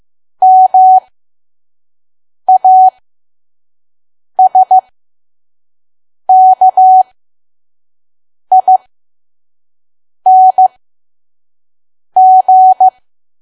Morse code, heard as a keyed tone, about 15 wpm, spaced as for 5 wpm, 750 Hz. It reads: MASKING
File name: Archive / 5 WPM files / PM